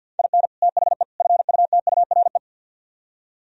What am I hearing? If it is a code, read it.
Morse code, heard as a keyed tone, about 62 wpm, 700 Hz. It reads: IN THE FUTURE
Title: Speed-Racing - Sets of 3 Words 50wpm